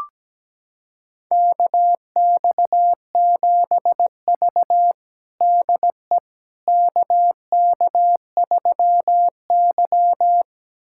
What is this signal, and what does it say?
Morse code, heard as a keyed tone, about 17 wpm, 700 Hz. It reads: KX7V DE KK3Y
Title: USA Callsign DE Callsign - 4 Characters - 17wpm